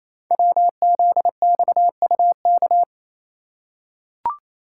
Morse code, heard as a keyed tone, about 27 wpm, 700 Hz. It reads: WZXUK
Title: Code Groups - 5 Letter:Numbers - 28wpm